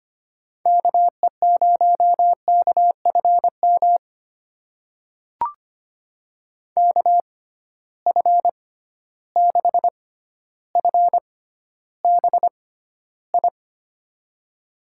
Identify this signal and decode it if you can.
Morse code, heard as a keyed tone, about 25 wpm, spaced as for 13 wpm, 700 Hz. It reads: KE0KFM KF6FBI
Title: USA Call Signs - 6 Characters - 6x Character Spacing - 25wpm